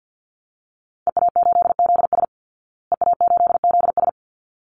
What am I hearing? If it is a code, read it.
Morse code, heard as a keyed tone, about 49 wpm, 700 Hz. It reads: EA8ZS EA8ZS
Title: Call Signs Repeated 3x - 50wpm